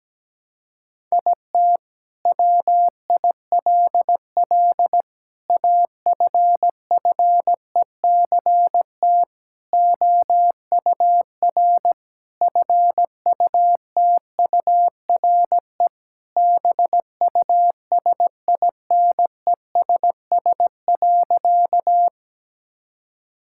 Morse code, heard as a keyed tone, about 17 wpm, 700 Hz. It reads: IT WILL AFFECT OUR FUTURE BUSINESS.
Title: Sentences from Top 1000 Words - Review Part 6 - 17wpm